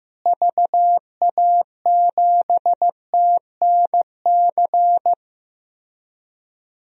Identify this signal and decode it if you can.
Morse code, heard as a keyed tone, about 15 wpm, 700 Hz. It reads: VA7TNC
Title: DXCC-100 Call Signs - 6 Characters - Repeated 3x - 15wpm